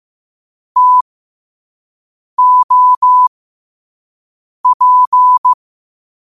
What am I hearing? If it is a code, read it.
Morse code, heard as a keyed tone, about 15 wpm, spaced as for 5 wpm, 1000 Hz. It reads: TOP
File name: qotdfiles / 5wpm